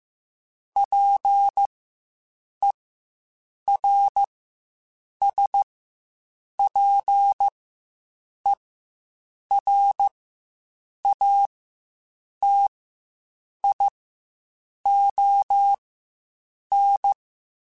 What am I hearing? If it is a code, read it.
Morse code, heard as a keyed tone, about 15 wpm, spaced as for 7 wpm, 800 Hz. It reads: PERSPERATION
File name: Quotes Spaced / Quotes Spaced-mp3s / Persperation sp @10